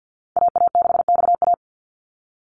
Morse code, heard as a keyed tone, about 50 wpm, 700 Hz. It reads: AA6XA
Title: NA SOTA Activator Callsigns 50wpm